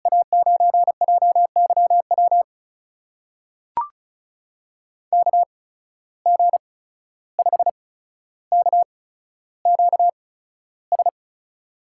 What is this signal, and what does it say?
Morse code, heard as a keyed tone, about 36 wpm, spaced as for 14 wpm, 700 Hz. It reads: A9JYW KG5KQS
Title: USA Call Signs - 6 Characters - 8x Character Spacing - 35wpm